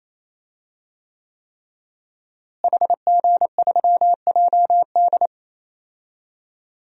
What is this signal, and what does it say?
Morse code, heard as a keyed tone, about 28 wpm, 700 Hz. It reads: HG3JD
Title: DXCC-100 Call Signs - 5 Characters - Repeated 3x - 28wpm